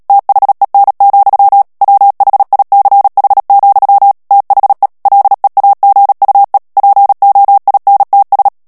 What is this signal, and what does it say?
Morse code, heard as a keyed tone, about 37 wpm, 800 Hz. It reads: THEN, WHICH, THE LEAGUE POINTS
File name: morse mp3 / ANGLAIS texte clair 3 vitesse 35 mots minute